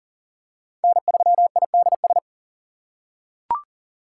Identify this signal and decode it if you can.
Morse code, heard as a keyed tone, about 39 wpm, 700 Hz. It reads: N3IDS
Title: USA Call Signs - 4x Character Spacing - 40wpm